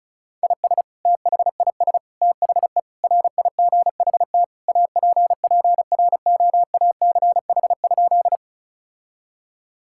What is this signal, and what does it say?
Morse code, heard as a keyed tone, about 36 wpm, 700 Hz. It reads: IS THIS THE RIGHT APPROACH?
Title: Sentences from Top 1000 Words - Review Part 6 - 35wpm